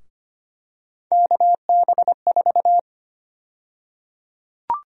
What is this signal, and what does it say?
Morse code, heard as a keyed tone, about 25 wpm, 700 Hz. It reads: KB4
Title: USA Call Sign Prefixes - Repeated 2x - 25wpm